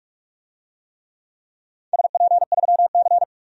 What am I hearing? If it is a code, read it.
Morse code, heard as a keyed tone, about 46 wpm, 700 Hz. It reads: SP3C
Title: DXCC-100 Call Signs - 4 Characters - 45wpm